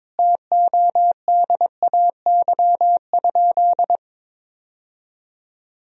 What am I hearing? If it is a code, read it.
Morse code, heard as a keyed tone, about 22 wpm, 700 Hz. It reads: TODAY?
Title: Sentences from Top 1000 Words - Review Part 1 - 22wpm